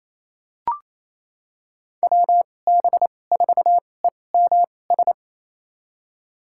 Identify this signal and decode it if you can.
Morse code, heard as a keyed tone, about 28 wpm, spaced as for 20 wpm, 700 Hz. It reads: WB4EMS
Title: USA Call Signs - 2x Character Spacing - 28wpm